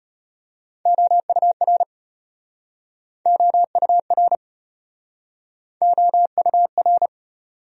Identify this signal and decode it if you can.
Morse code, heard as a keyed tone, about 37 wpm, 700 Hz. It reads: OUR OUR OUR